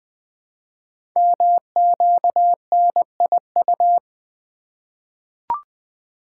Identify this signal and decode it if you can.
Morse code, heard as a keyed tone, about 20 wpm, 700 Hz. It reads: MQNIU
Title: Code Groups - 5 Letters - 20wpm